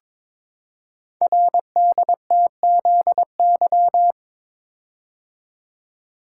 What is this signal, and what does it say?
Morse code, heard as a keyed tone, about 22 wpm, 700 Hz. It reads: RDTZY
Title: Code Groups - 5 Letters - 22wpm